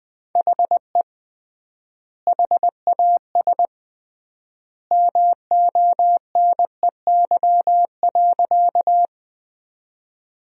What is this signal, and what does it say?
Morse code, heard as a keyed tone, about 20 wpm, 700 Hz. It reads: HE HAS MONEY.
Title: 3x Word-Spacing - Sentences from Top 400 Words - 20wpm